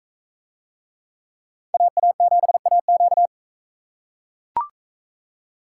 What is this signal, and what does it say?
Morse code, heard as a keyed tone, about 40 wpm, 700 Hz. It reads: AA7AQ